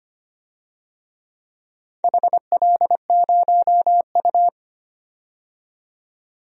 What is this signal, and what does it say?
Morse code, heard as a keyed tone, about 25 wpm, 700 Hz. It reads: HL0U
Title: DXCC-100 Call Signs - 4 Characters - 25wpm